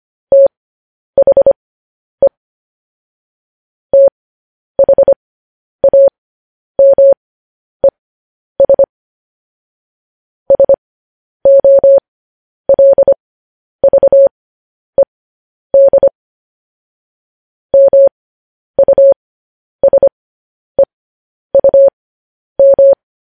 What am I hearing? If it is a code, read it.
Morse code, heard as a keyed tone, about 25 wpm, spaced as for 10 wpm, 550 Hz. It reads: THE THAMES SOLVED MUSEUM
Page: news-headlines-in-morse-code-at-10-wpm